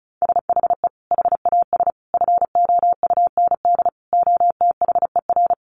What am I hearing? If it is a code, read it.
Morse code, heard as a keyed tone, about 34 wpm, 700 Hz. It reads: SHE HAS FOUND OTHER
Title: Sentences from Top 1500 Words 35wpm